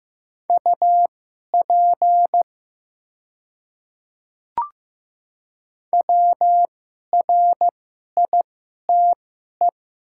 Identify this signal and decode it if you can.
Morse code, heard as a keyed tone, about 15 wpm, spaced as for 11 wpm, 700 Hz. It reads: UP WRITE
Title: 2x Character Spacing - Top 100 Words - 15wpm